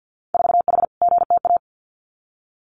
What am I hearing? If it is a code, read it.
Morse code, heard as a keyed tone, about 50 wpm, 700 Hz. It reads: 4H GTA
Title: Winter Field Day Exchanges Encore 50wpm